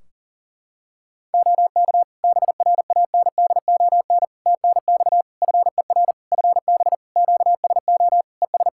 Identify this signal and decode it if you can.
Morse code, heard as a keyed tone, about 37 wpm, 700 Hz. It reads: OK BRANDON TNX FER FB QSO ES
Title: K1USN SST QSO Practice 40wpm